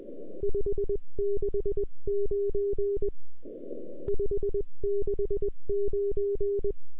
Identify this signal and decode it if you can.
Morse code, heard as a keyed tone, about 20 wpm, 400 Hz. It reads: 569 569